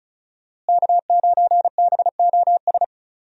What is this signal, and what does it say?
Morse code, heard as a keyed tone, about 34 wpm, 700 Hz. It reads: K9BOS
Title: USA Call Signs - 5 Characters - Repeated 3x - 35wpm